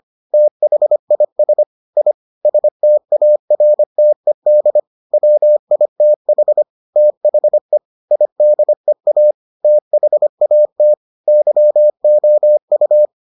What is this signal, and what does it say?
Morse code, heard as a keyed tone, about 25 wpm, 600 Hz. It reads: THIS I STARTED WITH THE IDEA THAT YOU